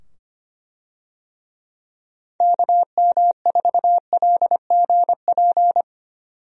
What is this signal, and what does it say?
Morse code, heard as a keyed tone, about 25 wpm, 700 Hz. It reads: KM4LGP
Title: USA Call Signs - 6 Characters - 25wpm